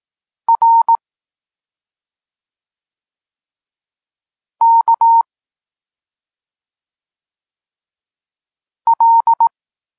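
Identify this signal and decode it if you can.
Morse code, heard as a keyed tone, about 18 wpm, 920 Hz. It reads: R K L
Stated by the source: Polish Morse lesson teaching the letter C, from SP9MOA Niepołomicki Klub Krótkofalowców.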